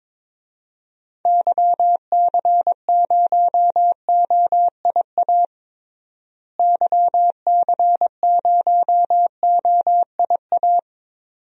Morse code, heard as a keyed tone, about 22 wpm, 700 Hz. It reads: YC0OIA YC0OIA
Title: DXCC-100 Call Signs - 6 Characters - Repeated 2x - 22wpm